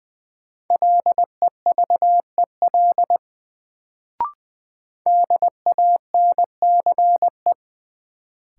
Morse code, heard as a keyed tone, about 20 wpm, 700 Hz. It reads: LEVEL DANCE
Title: Sentences from Top 700 Words - New Word Drill - 20wpm